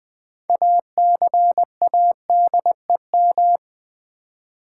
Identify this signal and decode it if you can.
Morse code, heard as a keyed tone, about 20 wpm, 700 Hz. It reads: ACADEM
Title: Build A Word - 3 Letters At A Time - 8-Letter Words - 20wpm